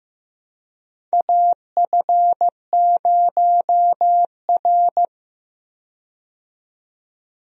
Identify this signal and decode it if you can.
Morse code, heard as a keyed tone, about 15 wpm, 700 Hz. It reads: AF0R
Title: USA Call Signs - 4 Characters - 15wpm